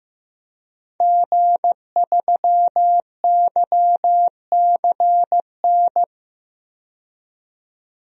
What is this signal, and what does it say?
Morse code, heard as a keyed tone, about 15 wpm, 700 Hz. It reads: G3YCN
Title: DXCC-100 Call Signs - 5 Characters - 15wpm